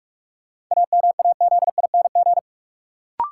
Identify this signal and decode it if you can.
Morse code, heard as a keyed tone, about 43 wpm, 700 Hz. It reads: AMAZING
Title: Sentences from Top 1500 Words - New Word Drill Alt - 45wpm